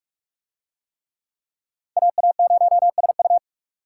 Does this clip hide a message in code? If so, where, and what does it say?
Yes: Morse code from 2.0 to 3.4 s, heard as a keyed tone, about 40 wpm, 700 Hz. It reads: AA0SU